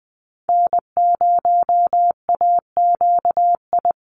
Morse code, heard as a keyed tone, about 20 wpm, 700 Hz. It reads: N0AQI
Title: USA Call Signs 20wpm